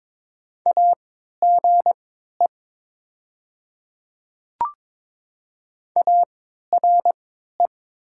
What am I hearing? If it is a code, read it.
Morse code, heard as a keyed tone, about 22 wpm, spaced as for 12 wpm, 700 Hz. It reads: AGE ARE
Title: 3x Character Spacing - Top 500 Words - 22wpm